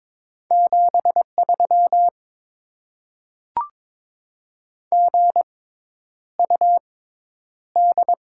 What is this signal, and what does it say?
Morse code, heard as a keyed tone, about 21 wpm, spaced as for 10 wpm, 700 Hz. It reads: 73 GUD
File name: Top 20 QSO Elements - 6x Character Spacing - 22wpm